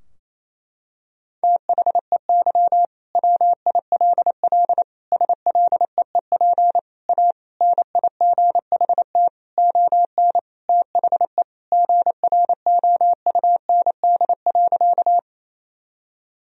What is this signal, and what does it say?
Morse code, heard as a keyed tone, about 28 wpm, 700 Hz. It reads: THEY WILL SLEEP A NIGHT ON THE GROUND.
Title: Sentences from Top 1000 Words - Review Part 2 - 28wpm